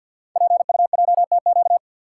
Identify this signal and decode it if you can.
Morse code, heard as a keyed tone, about 48 wpm, 700 Hz. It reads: PUJTQ
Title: Code Groups - 5 Letter:Numbers - 50wpm